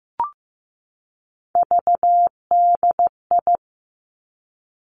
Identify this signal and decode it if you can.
Morse code, heard as a keyed tone, about 15 wpm, 700 Hz. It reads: VDI
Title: Three Challenging Letter-Number Combinations 15wpm